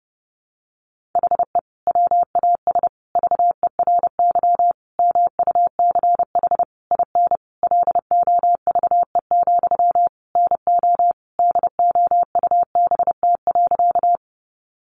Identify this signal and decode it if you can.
Morse code, heard as a keyed tone, about 30 wpm, 700 Hz. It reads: HE WAS VERY MUCH IN LOVE, NO DOUBT.
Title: Sentences from Top 1500 Words 30wpm